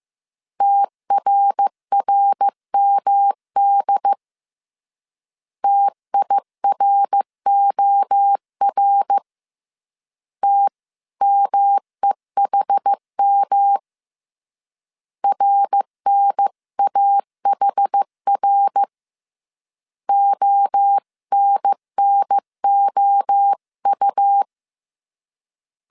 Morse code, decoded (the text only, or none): TRRMD TIROR TMEHM RNAHR ONNOU